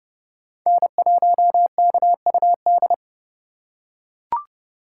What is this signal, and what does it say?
Morse code, heard as a keyed tone, about 29 wpm, 700 Hz. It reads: N1KUD
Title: USA Call Signs - 5 Characters - 2x Character Spacing - 30wpm